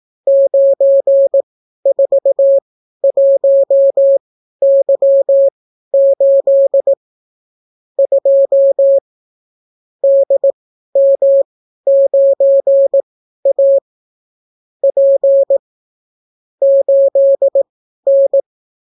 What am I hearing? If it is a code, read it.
Morse code, heard as a keyed tone, about 18 wpm, spaced as for 12 wpm, 550 Hz. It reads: 941Y8 2 DM9A P 8N